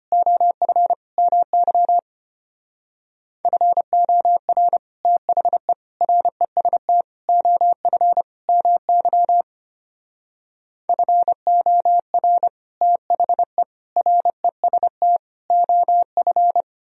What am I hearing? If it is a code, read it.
Morse code, heard as a keyed tone, about 33 wpm, 700 Hz. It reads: OF MY FOR THE REST OF MY FOR THE REST OF